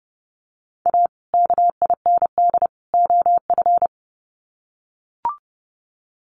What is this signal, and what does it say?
Morse code, heard as a keyed tone, about 30 wpm, 700 Hz. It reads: A KIND OF
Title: Sets of 3 Words 30wpm